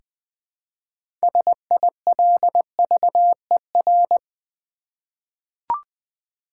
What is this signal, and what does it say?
Morse code, heard as a keyed tone, about 20 wpm, 700 Hz. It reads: SILVER
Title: Six Letter Words - 20wpm